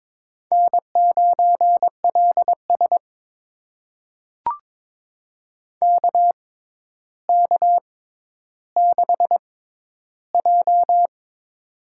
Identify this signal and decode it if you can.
Morse code, heard as a keyed tone, about 22 wpm, spaced as for 11 wpm, 700 Hz. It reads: N9LS KK6J
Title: USA Call Signs - 4 Characters - 6x Character Spacing - 22wpm